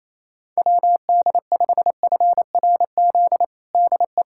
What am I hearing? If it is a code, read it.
Morse code, heard as a keyed tone, about 27 wpm, 700 Hz. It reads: WD5FRZ DE
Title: USA Callsign DE Callsign 28wpm